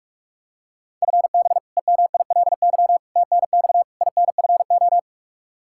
Morse code, heard as a keyed tone, about 42 wpm, 700 Hz. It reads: FB EMILY TNX INFO
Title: Practice QSO - Exchange 3 - More Info - 45wpm